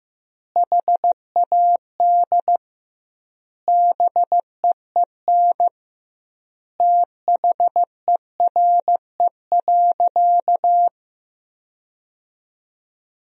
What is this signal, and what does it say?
Morse code, heard as a keyed tone, about 15 wpm, 700 Hz. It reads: HAD BEEN THERE.
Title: 2x Word-Spacing - Sentences from Top 300 Words - 15wpm